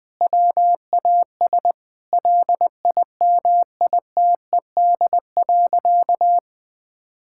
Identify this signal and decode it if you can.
Morse code, heard as a keyed tone, about 20 wpm, 700 Hz. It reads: WAS LIMITED.
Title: Sentences from Top 1000 Words - Review Part 6 - 20wpm